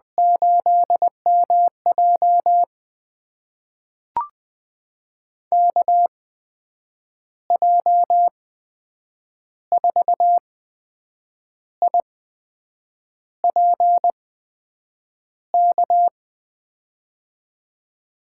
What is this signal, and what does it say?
Morse code, heard as a keyed tone, about 20 wpm, spaced as for 7 wpm, 700 Hz. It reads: 8MJ KJ4IPK
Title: USA Call Signs - 8x Character Spacing - 20wpm